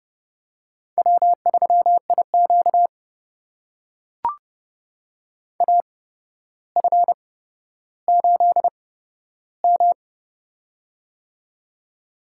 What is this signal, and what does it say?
Morse code, heard as a keyed tone, about 30 wpm, spaced as for 13 wpm, 700 Hz. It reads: W3IQ AF8M
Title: USA Call Signs - 8x Character Spacing - 30wpm